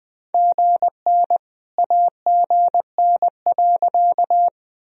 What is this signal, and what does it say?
Morse code, heard as a keyed tone, about 21 wpm, 700 Hz. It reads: GN AGN.
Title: Sentences from Top 300 Words and Abbreviations 20wpm